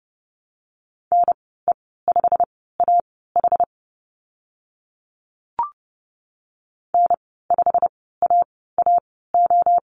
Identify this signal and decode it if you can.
Morse code, heard as a keyed tone, about 30 wpm, spaced as for 17 wpm, 700 Hz. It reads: NE5AH N5AAO
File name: Lesson 15g - Number 5 - Call Signs 30wpm